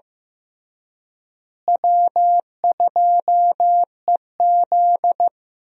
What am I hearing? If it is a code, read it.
Morse code, heard as a keyed tone, about 15 wpm, 700 Hz. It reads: W2EZ
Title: USA Call Signs - 4 Characters - Repeated 3x - 15wpm